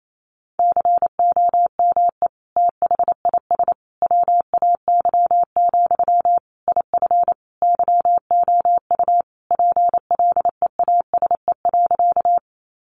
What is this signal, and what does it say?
Morse code, heard as a keyed tone, about 28 wpm, 700 Hz. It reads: COME THIS WAY, IF YOU PLEASE.
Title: Sentences from Top 1500 Words 28wpm